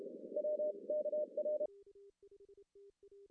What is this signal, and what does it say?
Morse code, heard as a keyed tone, about 30 wpm, 600 Hz. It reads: WKR
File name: CW / cqww-cw-k5tr